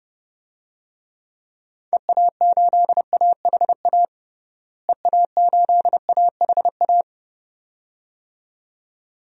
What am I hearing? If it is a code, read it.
Morse code, heard as a keyed tone, about 30 wpm, 700 Hz. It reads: EA8AHA EA8AHA
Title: DXCC-100 Call Signs - 6 Characters - Repeated 2x - 30wpm